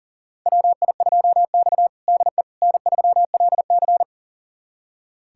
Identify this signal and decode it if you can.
Morse code, heard as a keyed tone, about 40 wpm, 700 Hz. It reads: WI2X DE N3LC
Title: USA Callsign DE Callsign - 4 Characters - 40wpm